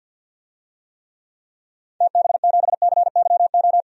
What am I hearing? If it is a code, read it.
Morse code, heard as a keyed tone, about 52 wpm, 700 Hz. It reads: TB7CYO
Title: DXCC-100 Call Signs - 6 Characters - Repeated 3x - 50wpm